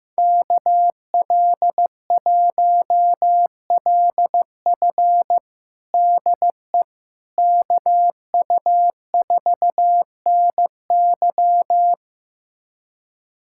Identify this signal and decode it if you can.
Morse code, heard as a keyed tone, about 15 wpm, 700 Hz. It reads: KL1LF DE KU4NY
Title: USA Callsign DE Callsign - 5 Characters - 15wpm